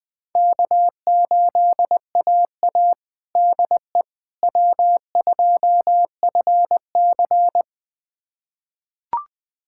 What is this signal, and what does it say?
Morse code, heard as a keyed tone, about 20 wpm, 700 Hz. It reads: K8AA DE W2FC
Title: USA Callsign DE Callsign - 4 Characters - 20wpm